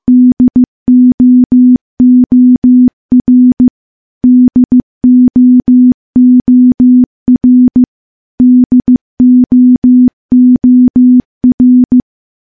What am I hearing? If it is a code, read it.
Morse code, heard as a keyed tone, about 15 wpm, 250 Hz. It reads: DOOR DOOR DOOR
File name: morse-training-15WPM